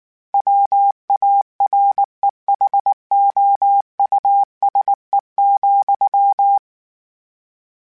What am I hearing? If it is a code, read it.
Morse code, heard as a keyed tone, about 19 wpm, 800 Hz. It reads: WAREHOUSE,